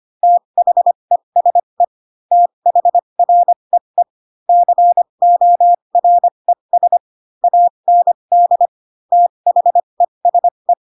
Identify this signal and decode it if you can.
Morse code, heard as a keyed tone, about 25 wpm, spaced as for 22 wpm, 700 Hz. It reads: THESE THREE CORES AND THESE
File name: Pinocchio / Pinocchio ch7 18 25 ik1hge